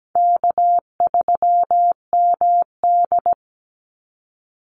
Morse code, heard as a keyed tone, about 17 wpm, 700 Hz. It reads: K3MD
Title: Easy Call Signs - 17wpm